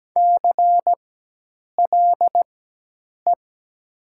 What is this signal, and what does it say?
Morse code, heard as a keyed tone, about 17 wpm, spaced as for 8 wpm, 700 Hz. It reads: CLE
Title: 4x Character Spacing - Top 500 Words - 17wpm